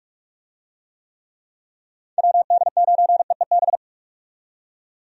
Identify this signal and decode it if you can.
Morse code, heard as a keyed tone, about 46 wpm, 700 Hz. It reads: WD9EEB
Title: USA Call Signs - 6 Characters - 45wpm